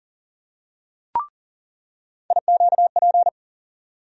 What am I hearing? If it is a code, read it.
Morse code, heard as a keyed tone, about 39 wpm, 700 Hz. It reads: IQP
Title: USA Call Sign Postfixes - 3 Letters 40wpm